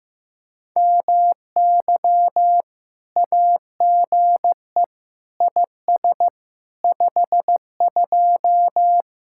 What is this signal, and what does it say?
Morse code, heard as a keyed tone, about 15 wpm, 700 Hz. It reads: MY AGE IS 52
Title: Speed-Racing Common QSO Phrases Encore - 15wpm